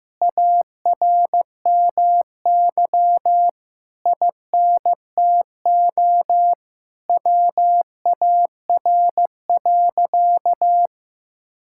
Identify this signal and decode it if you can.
Morse code, heard as a keyed tone, about 15 wpm, 700 Hz. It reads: ARMY INTO WAR.
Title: Sentences from Top 1000 Words - Review Part 5 - 15wpm